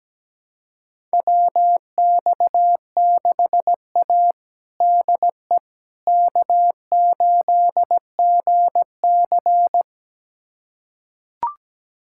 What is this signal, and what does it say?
Morse code, heard as a keyed tone, about 17 wpm, 700 Hz. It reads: WX6A DE K8GC
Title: USA Callsign DE Callsign - 4 Characters - 17wpm